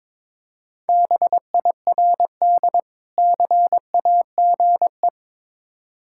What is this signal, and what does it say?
Morse code, heard as a keyed tone, about 22 wpm, 690 Hz. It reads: BIRD CAGE
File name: Difficult Words with Hints - 22wpm